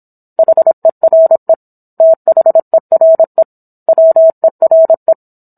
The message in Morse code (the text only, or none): HERE THERE WERE